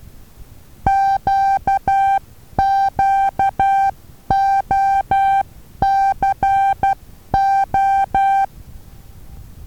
Morse code, heard as a keyed tone, about 10 wpm, 780 Hz. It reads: QQOCO
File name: Taviro HA6VA A